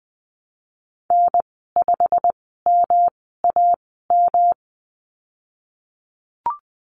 Morse code, heard as a keyed tone, about 20 wpm, spaced as for 15 wpm, 700 Hz. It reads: N5MAM
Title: 173 Lesson 28g - Number 9 - Call Signs 20wpm